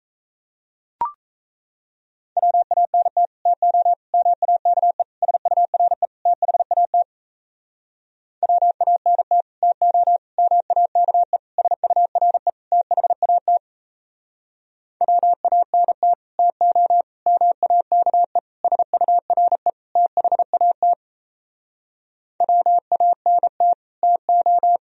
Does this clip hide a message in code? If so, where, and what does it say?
Yes: Morse code from 2.4 to 24.8 s, heard as a keyed tone, about 38 wpm, 700 Hz. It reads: WANT TO MAKE SURE THAT WANT TO MAKE SURE THAT WANT TO MAKE SURE THAT WANT TO